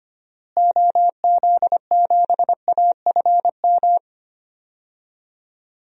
Morse code, heard as a keyed tone, about 25 wpm, 690 Hz. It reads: OZ7AFM
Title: DXCC-100 Call Signs - 6 Characters - Repeated 3x - 25wpm